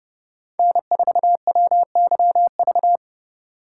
Morse code, heard as a keyed tone, about 36 wpm, 700 Hz. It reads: N4WYV
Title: Speed Racing - USA Call Signs 30wpm